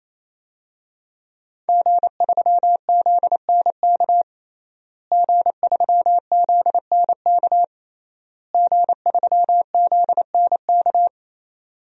Morse code, heard as a keyed tone, about 28 wpm, 700 Hz. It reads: G3ZNK G3ZNK G3ZNK
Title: DXCC-100 Call Signs - 5 Characters - Repeated 3x - 28wpm